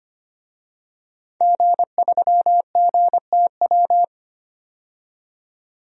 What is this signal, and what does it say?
Morse code, heard as a keyed tone, about 25 wpm, 700 Hz. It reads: G3GTW
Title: DXCC-100 Call Signs - 5 Characters - 25wpm